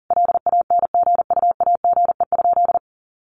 Morse code, heard as a keyed tone, about 40 wpm, 700 Hz. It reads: LANGUAGE?
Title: Sentences from Top 1100 Words 40wpm